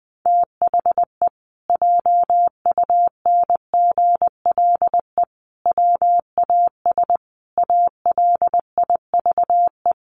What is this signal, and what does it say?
Morse code, heard as a keyed tone, about 20 wpm, 700 Hz. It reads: THE JUNGLE WAS ALIVE
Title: A Sound Of Thunder 20wpm